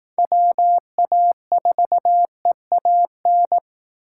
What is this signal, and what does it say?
Morse code, heard as a keyed tone, about 18 wpm, 700 Hz. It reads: WA4EAN
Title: Speed Racing - USA 2x3 Call Signs 15wpm